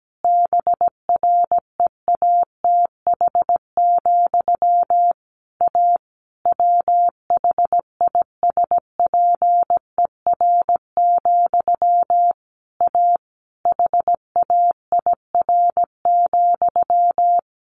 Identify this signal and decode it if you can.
Morse code, heard as a keyed tone, about 17 wpm, 700 Hz. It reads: BREATH, A WHISPER, A HAIR,
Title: A Sound Of Thunder 17wpm